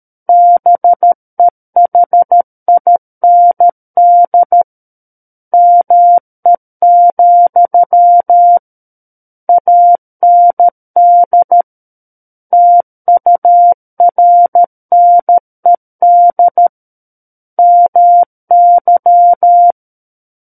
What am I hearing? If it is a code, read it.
Morse code, heard as a keyed tone, about 13 wpm, 700 Hz. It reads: BEHIND ME, AND TURNED MY
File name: Book One / 04-Chapter 4 - 13 WPM